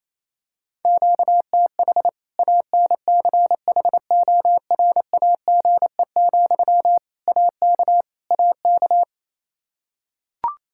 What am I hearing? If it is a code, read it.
Morse code, heard as a keyed tone, about 28 wpm, 700 Hz. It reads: QTH ANCHORAGE, AK AK
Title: Practice QSO - Exchange 1 - Beginning - 28wpm